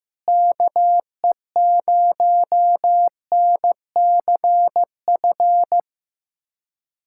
Morse code, heard as a keyed tone, about 15 wpm, 700 Hz. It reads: KE0NCF
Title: USA Callsign DE Callsign 15wpm